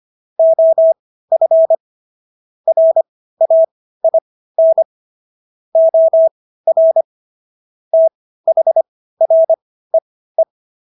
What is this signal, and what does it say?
Morse code, heard as a keyed tone, about 25 wpm, spaced as for 15 wpm, 650 Hz. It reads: OF RAIN OR THREE
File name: SS110 15